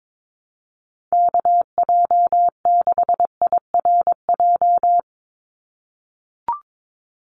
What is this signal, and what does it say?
Morse code, heard as a keyed tone, about 22 wpm, 700 Hz. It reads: KJ6IRJ
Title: USA Call Signs 22wpm